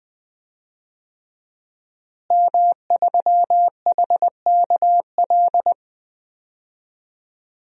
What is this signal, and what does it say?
Morse code, heard as a keyed tone, about 20 wpm, 700 Hz. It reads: M3HKL
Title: DXCC-100 Call Signs - 5 Characters - Repeated 2x - 20wpm